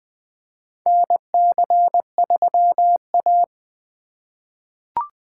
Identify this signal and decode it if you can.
Morse code, heard as a keyed tone, about 20 wpm, 700 Hz. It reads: NC3A
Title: USA Call Signs - 2x Character Spacing - 20wpm